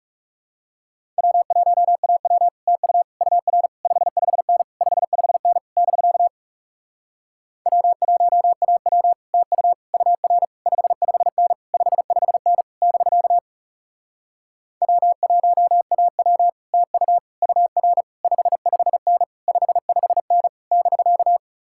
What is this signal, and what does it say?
Morse code, heard as a keyed tone, about 37 wpm, 700 Hz. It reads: W1AW TU UR 55N 55N <BK> W1AW TU UR 55N 55N <BK> W1AW TU UR 55N 55N <BK>
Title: Speed-Racing Common QSO Phrases Encore - 30wpm